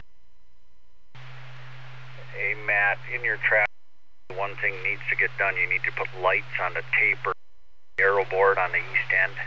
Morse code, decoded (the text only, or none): none